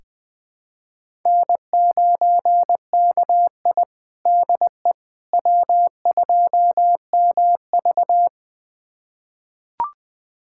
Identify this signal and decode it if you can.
Morse code, heard as a keyed tone, about 20 wpm, 700 Hz. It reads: N9KI DE W2MV
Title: USA Callsign DE Callsign - 4 Characters - 20wpm